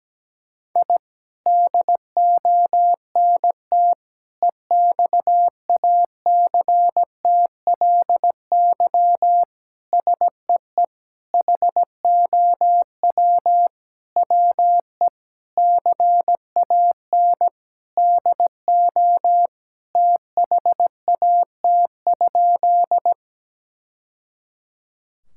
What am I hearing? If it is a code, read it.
Morse code, heard as a keyed tone, about 17 wpm, 700 Hz. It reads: I DONT EXACTLY SEE HOW WE CAN DO THAT?
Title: Sentences from Top 1000 Words - Review Part 6 - 17wpm